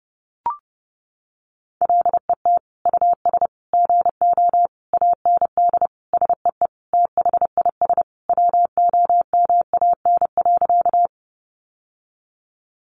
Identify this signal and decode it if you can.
Morse code, heard as a keyed tone, about 30 wpm, 700 Hz. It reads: LET US GO AND SEE THIS WOMAN.
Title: Sentences from Top 1100 Words 30wpm